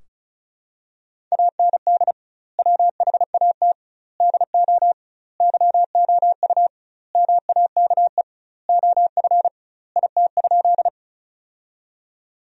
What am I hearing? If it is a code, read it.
Morse code, heard as a keyed tone, about 35 wpm, 700 Hz. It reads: AND WHAT DO YOU MAKE OF IT?
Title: 2x Word-Spacing - Sentences from Top 100 Words - 35wpm